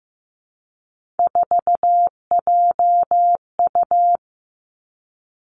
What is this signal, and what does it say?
Morse code, heard as a keyed tone, about 15 wpm, 700 Hz. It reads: 4JU
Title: Three Challenging Letter-Number Combinations 15wpm